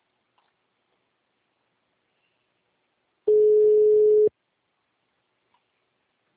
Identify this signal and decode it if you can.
Ringback tone (Europe, 425 Hz).